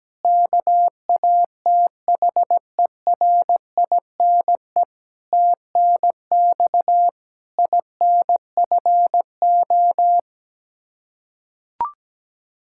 Morse code, heard as a keyed tone, about 17 wpm, 700 Hz. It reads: KATHERINE TNX INFO
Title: Practice QSO - Exchange 3 - More Info - 17wpm